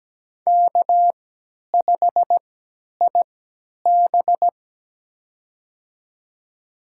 Morse code, heard as a keyed tone, about 17 wpm, spaced as for 10 wpm, 700 Hz. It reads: K5IB
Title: USA Call Signs - 4 Characters - 3x Character Spacing - 17wpm